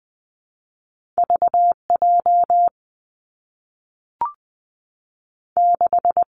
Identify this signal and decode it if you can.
Morse code, heard as a keyed tone, about 19 wpm, 700 Hz. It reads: V J 6
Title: Two Challenging Letter-Number Combinations 20wpm